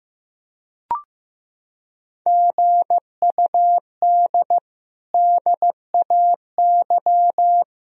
Morse code, heard as a keyed tone, about 14 wpm, 700 Hz. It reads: GUD DAY
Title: Sentences from Top 200 Words and Abbreviations 15wpm